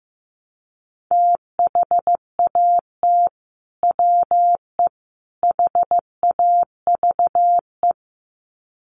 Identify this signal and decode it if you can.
Morse code, heard as a keyed tone, about 15 wpm, 700 Hz. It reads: THAT WE HAVE
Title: Sets of 3 Words 15wpm